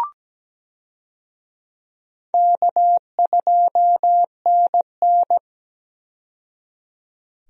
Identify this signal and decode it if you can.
Morse code, heard as a keyed tone, about 17 wpm, 700 Hz. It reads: K2NN
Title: USA Call Signs - 4 Characters - 17wpm